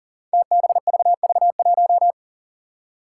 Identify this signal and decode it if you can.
Morse code, heard as a keyed tone, about 40 wpm, 700 Hz. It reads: TBVV1
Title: Five Challenging Letter-Number Combinations 40wpm